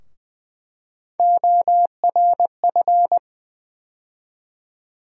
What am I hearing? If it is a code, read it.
Morse code, heard as a keyed tone, about 20 wpm, 700 Hz. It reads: ORF
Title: USA Call Sign Postfixes - 3 Letters 20wpm